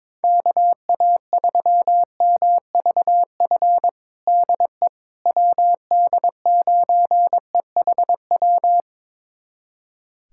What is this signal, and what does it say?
Morse code, heard as a keyed tone, about 22 wpm, 700 Hz. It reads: KA3MVF DE WD9EHW
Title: USA Callsign DE Callsign 22wpm